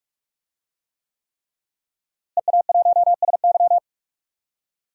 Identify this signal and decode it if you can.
Morse code, heard as a keyed tone, about 42 wpm, 700 Hz. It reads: EA1SY
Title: DXCC-100 Call Signs - 5 Characters - Repeated 3x - 45wpm